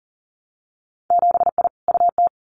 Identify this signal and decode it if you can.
Morse code, heard as a keyed tone, about 41 wpm, 700 Hz. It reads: 7I UT
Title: Winter Field Day Exchanges Encore 40wpm